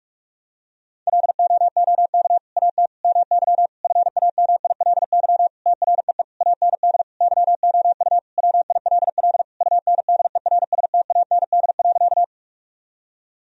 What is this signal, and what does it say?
Morse code, heard as a keyed tone, about 41 wpm, 700 Hz. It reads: LOOK AT MY FAMILY TREE AND YOU WILL UNDERSTAND.
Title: Sentences from Top 1000 Words - Review Part 1 - 45wpm